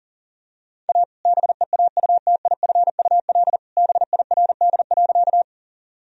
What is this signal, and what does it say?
Morse code, heard as a keyed tone, about 37 wpm, 700 Hz. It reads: A BEAUTIFUL BIRD.
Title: Sentences from Top 1000 Words - Review Part 2 - 40wpm